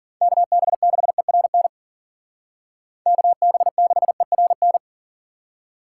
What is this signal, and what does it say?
Morse code, heard as a keyed tone, about 52 wpm, 700 Hz. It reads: KB6ERN KB6ERN